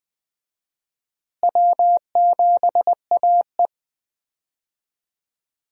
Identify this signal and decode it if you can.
Morse code, heard as a keyed tone, about 20 wpm, 700 Hz. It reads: W7AE
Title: USA Call Signs - 4 Characters - Repeated 2x - 20wpm